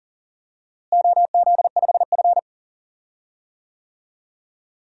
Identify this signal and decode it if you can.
Morse code, heard as a keyed tone, about 40 wpm, 700 Hz. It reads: OZ5F
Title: DXCC-100 Call Signs - 4 Characters - Repeated 3x - 40wpm